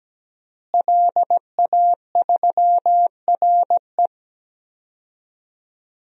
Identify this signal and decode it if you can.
Morse code, heard as a keyed tone, about 17 wpm, 700 Hz. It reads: LA3RE
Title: DXCC-100 Call Signs - 5 Characters - Repeated 2x - 17wpm